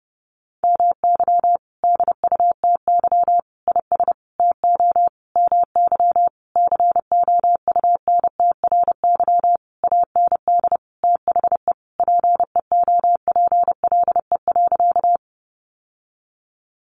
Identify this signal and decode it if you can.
Morse code, heard as a keyed tone, about 30 wpm, 700 Hz. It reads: MY DUTY IS TO MY COUNTRY AND THE PEOPLE.
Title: Sentences from Top 1500 Words 30wpm